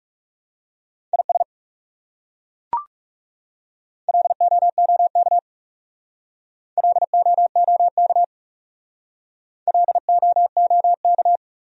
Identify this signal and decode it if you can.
Morse code, heard as a keyed tone, about 40 wpm, 700 Hz. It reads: IS LOOK LOOK LOOK